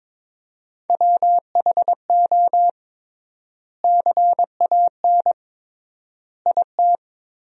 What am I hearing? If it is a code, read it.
Morse code, heard as a keyed tone, about 22 wpm, 700 Hz. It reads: WHO CAN IT
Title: 3x Word-Spacing - Sentences from Top 100 Words - 22wpm